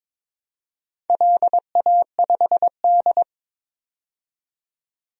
Morse code, heard as a keyed tone, about 22 wpm, 700 Hz. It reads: LA5D
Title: DXCC-100 Call Signs - 4 Characters - 22wpm